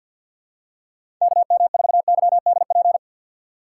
Morse code, heard as a keyed tone, about 49 wpm, 700 Hz. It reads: KM4YDP
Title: USA Call Signs - 6 Characters - 4x Character Spacing - 50wpm